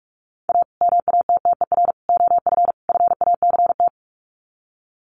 Morse code, heard as a keyed tone, about 44 wpm, 700 Hz. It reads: A MATTER OF FACT
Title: Sets of 4 Words 45wpm